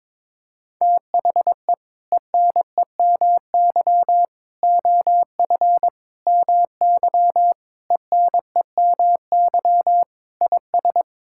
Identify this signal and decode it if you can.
Morse code, heard as a keyed tone, about 21 wpm, 700 Hz. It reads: THE ENEMY OF MY ENEMY IS
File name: practice / 22wpm / Sets of 8 Words - 22wpm